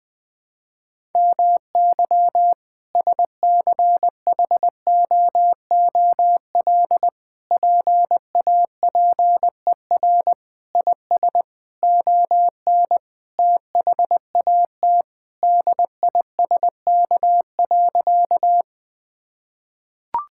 Morse code, heard as a keyed tone, about 20 wpm, 700 Hz. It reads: MY SCHOOL PAPER IS ON THAT DISK.
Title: Sentences from Top 1000 Words - Review Part 6 - 20wpm